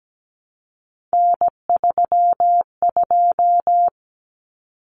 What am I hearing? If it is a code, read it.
Morse code, heard as a keyed tone, about 17 wpm, 700 Hz. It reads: N32
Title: Common Cut Numbers - 3-Digit 17wpm